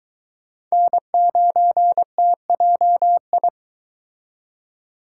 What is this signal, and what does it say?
Morse code, heard as a keyed tone, about 23 wpm, 700 Hz. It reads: N9TJI